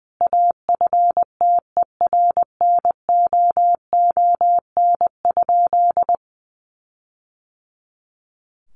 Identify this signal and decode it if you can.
Morse code, heard as a keyed tone, about 20 wpm, 700 Hz. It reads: AFTERNOON?
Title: Sentences from Top 1500 Words 20wpm